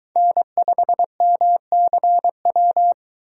Morse code, heard as a keyed tone, about 22 wpm, 700 Hz. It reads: N5MCW